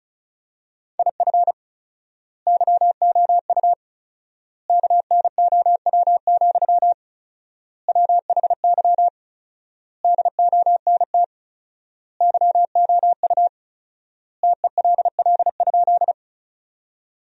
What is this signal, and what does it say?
Morse code, heard as a keyed tone, about 34 wpm, 700 Hz. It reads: IF YOU KNOW, WHY DONT YOU TELL?
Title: 4x Word-Spacing - Sentences from Top 300 Words - 35wpm